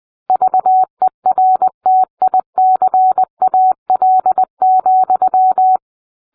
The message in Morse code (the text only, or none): VERTICAL,